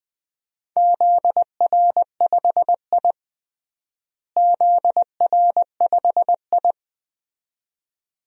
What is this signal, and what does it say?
Morse code, heard as a keyed tone, about 20 wpm, 700 Hz. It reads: ZR5I ZR5I
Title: DXCC-100 Call Signs - 4 Characters - Repeated 2x - 20wpm